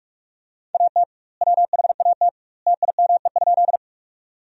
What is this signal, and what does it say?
Morse code, heard as a keyed tone, about 46 wpm, 700 Hz. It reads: AT WHAT TIME?
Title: 2x Word-Spacing - Sentences from Top 100 Words - 45wpm